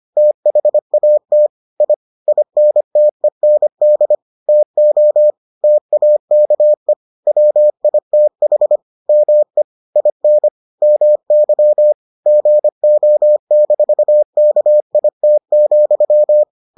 Morse code, heard as a keyed tone, about 25 wpm, 600 Hz. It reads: THAT I INTEND TO TAKE WITH ME IN MY GO-KIT,